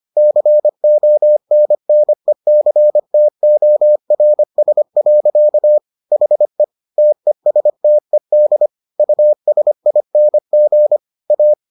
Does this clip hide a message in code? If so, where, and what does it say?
Yes: Morse code from 0.2 to 11.5 s, heard as a keyed tone, about 25 wpm, 600 Hz. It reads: CONNECTORS. HE TESTED USING A